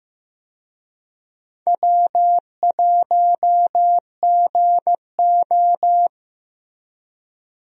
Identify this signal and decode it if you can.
Morse code, heard as a keyed tone, about 15 wpm, 700 Hz. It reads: W1GO
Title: USA Call Signs - 4 Characters - Repeated 3x - 15wpm